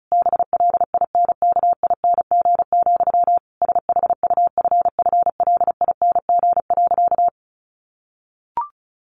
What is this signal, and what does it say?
Morse code, heard as a keyed tone, about 34 wpm, 700 Hz. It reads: BLINKING, SHUFFLING.
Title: A Sound Of Thunder 35wpm